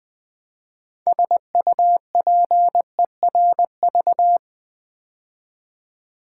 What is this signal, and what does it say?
Morse code, heard as a keyed tone, about 20 wpm, 700 Hz. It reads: SUPERV
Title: Build A Word - 3 Letters At A Time - 9-Letter Words - 20wpm